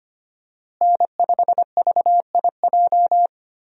Morse code, heard as a keyed tone, about 25 wpm, 690 Hz. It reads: N5VIJ
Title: USA Call Signs - 5 Characters - Repeated 2x - 25wpm